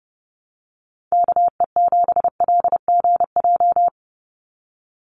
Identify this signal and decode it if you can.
Morse code, heard as a keyed tone, about 30 wpm, 700 Hz. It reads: KE7LGJ
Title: USA Call Signs 30wpm